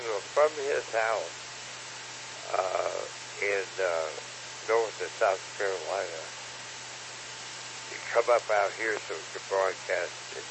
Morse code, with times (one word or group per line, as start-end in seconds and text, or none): none